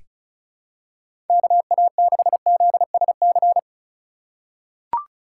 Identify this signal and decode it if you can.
Morse code, heard as a keyed tone, about 36 wpm, 700 Hz. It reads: KA6ZSC
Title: USA Call Signs - 6 Characters - 2x Character Spacing - 35wpm